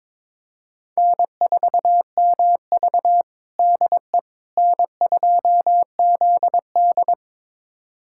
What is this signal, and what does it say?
Morse code, heard as a keyed tone, about 22 wpm, 700 Hz. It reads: N4MV DE N2ZD
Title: USA Callsign DE Callsign - 4 Characters - 22wpm